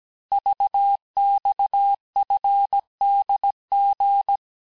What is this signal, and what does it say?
Morse code, heard as a keyed tone, about 17 wpm, 800 Hz. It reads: VXFDG
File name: cours-du-lundi files / cours-1587-texte-chiffre-vitesse-16MPM